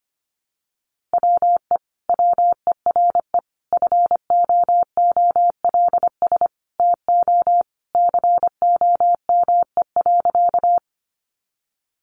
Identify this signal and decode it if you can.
Morse code, heard as a keyed tone, about 25 wpm, 700 Hz. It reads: WE WERE FOOLS TO COME.
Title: A Sound Of Thunder 25wpm